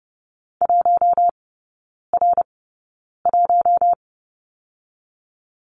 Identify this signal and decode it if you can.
Morse code, heard as a keyed tone, about 30 wpm, spaced as for 9 wpm, 700 Hz. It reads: 1R1
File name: 081 Lesson 14e - Number 2 - Three Characters 30wpm